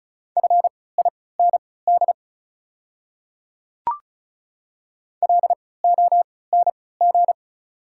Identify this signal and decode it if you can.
Morse code, heard as a keyed tone, about 36 wpm, spaced as for 20 wpm, 690 Hz. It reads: FIND LONG
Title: 3x Character Spacing - Top 100 Words - 35wpm